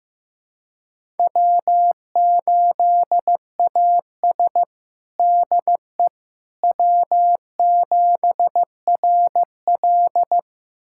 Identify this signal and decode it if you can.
Morse code, heard as a keyed tone, about 16 wpm, 700 Hz. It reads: W8AS DE W7RL
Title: USA Callsign DE Callsign - 4 Characters - 15wpm